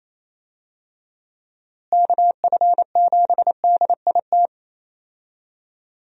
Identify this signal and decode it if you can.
Morse code, heard as a keyed tone, about 28 wpm, 700 Hz. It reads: KF7DIT
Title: USA Call Signs - 6 Characters - Repeated 3x - 28wpm